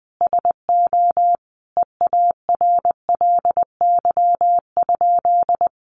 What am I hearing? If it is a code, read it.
Morse code, heard as a keyed tone, about 19 wpm, 700 Hz. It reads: SO EARLY?
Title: Sentences from Top 400 Words 20wpm